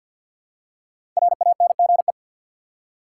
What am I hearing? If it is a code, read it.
Morse code, heard as a keyed tone, about 47 wpm, 700 Hz. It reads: RANGE
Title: Five Letter Words - 50wpm